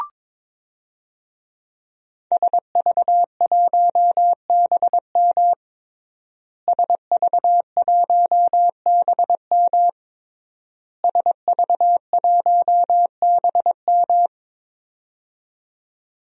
Morse code, heard as a keyed tone, about 22 wpm, 700 Hz. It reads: SV1BM SV1BM SV1BM
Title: DXCC-100 Call Signs - 5 Characters - Repeated 3x - 22wpm